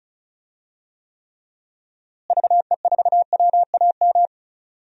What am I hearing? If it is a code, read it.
Morse code, heard as a keyed tone, about 36 wpm, 700 Hz. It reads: VE4WAM
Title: DXCC-100 Call Signs - 6 Characters - Repeated 3x - 35wpm